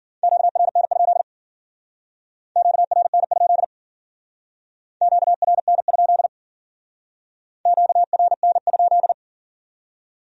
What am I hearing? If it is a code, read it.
Morse code, heard as a keyed tone, about 50 wpm, 700 Hz. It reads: QRN? QRN? QRN? QRN?
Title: Speed-Racing Q-Codes - 40wpm